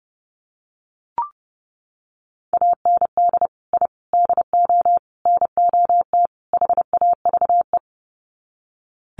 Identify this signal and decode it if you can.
Morse code, heard as a keyed tone, about 30 wpm, 700 Hz. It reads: AND I DO NOT HAVE
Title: Sets of 5 Words Encore 30wpm